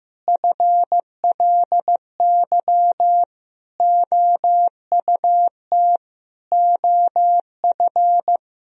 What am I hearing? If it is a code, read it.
Morse code, heard as a keyed tone, about 15 wpm, 700 Hz. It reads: FLY OUT OF
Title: Sentences from Top 1000 Words - Review Part 2 - 15wpm